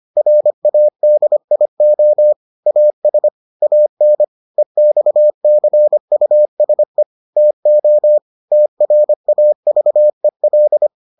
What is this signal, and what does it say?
Morse code, heard as a keyed tone, about 25 wpm, 600 Hz. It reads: RADIO AS AN EXCUSE TO TRAVEL